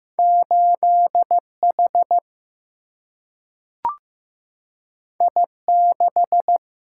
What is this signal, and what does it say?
Morse code, heard as a keyed tone, about 16 wpm, 700 Hz. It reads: 8H I6
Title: Single-Letter Single-Number 15wpm